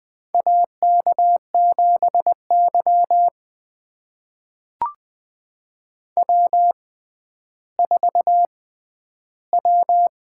Morse code, heard as a keyed tone, about 20 wpm, spaced as for 11 wpm, 700 Hz. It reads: AK7Y W4W
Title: USA Call Signs - 4 Characters - 6x Character Spacing - 20wpm